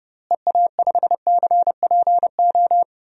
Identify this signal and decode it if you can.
Morse code, heard as a keyed tone, about 30 wpm, 700 Hz. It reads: EA5CPO
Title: DXCC-100 Call Signs - 6 Characters - Repeated 2x - 30wpm